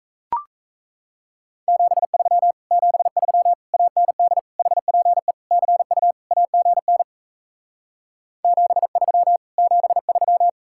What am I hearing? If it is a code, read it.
Morse code, heard as a keyed tone, about 37 wpm, 700 Hz. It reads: 73 73 AND HPE CU AGN 73 73